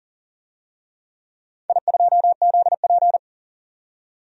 Morse code, heard as a keyed tone, about 40 wpm, 700 Hz. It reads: I2ZP
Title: DXCC-100 Call Signs - 4 Characters - 40wpm